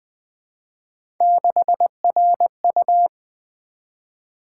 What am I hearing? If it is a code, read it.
Morse code, heard as a keyed tone, about 20 wpm, 700 Hz. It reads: 6RU
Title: Three Letter-Number Combinations - 20wpm